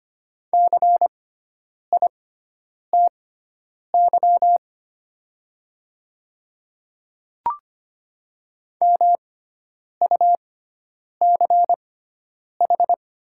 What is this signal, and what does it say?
Morse code, heard as a keyed tone, about 25 wpm, spaced as for 9 wpm, 690 Hz. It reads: CITY MUCH
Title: 6x Character Spacing - Top 300 Words - 25wpm